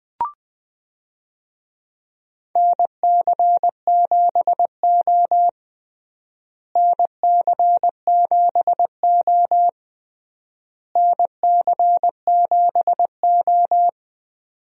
Morse code, heard as a keyed tone, about 21 wpm, 700 Hz. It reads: NC7O NC7O NC7O
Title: USA Call Signs - 4 Characters - Repeated 3x - 20wpm